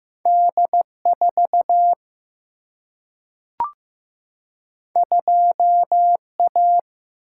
Morse code, heard as a keyed tone, about 15 wpm, 700 Hz. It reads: D 4 2 A
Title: Single-Letter Single-Number 15wpm